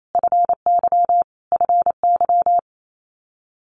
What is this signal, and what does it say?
Morse code, heard as a keyed tone, about 28 wpm, 700 Hz. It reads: FY FY
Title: Multi-letter Phonemes Repeated 3x 28wpm